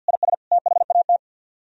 Morse code, heard as a keyed tone, about 63 wpm, 700 Hz. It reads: IS THAT